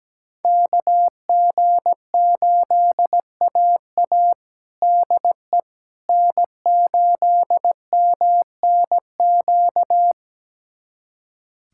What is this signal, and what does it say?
Morse code, heard as a keyed tone, about 17 wpm, 700 Hz. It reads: KG8AA DE N8MNQ
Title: USA Callsign DE Callsign - 5 Characters - 17wpm